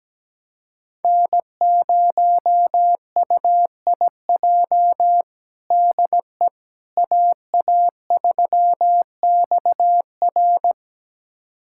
Morse code, heard as a keyed tone, about 17 wpm, 700 Hz. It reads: N0UIJ DE AA3XR
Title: USA Callsign DE Callsign - 5 Characters - 17wpm